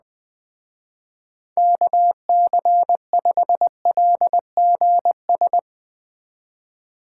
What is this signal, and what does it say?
Morse code, heard as a keyed tone, about 20 wpm, 700 Hz. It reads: KC5LGS
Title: Speed Racing - USA 2x3 Call Signs 17wpm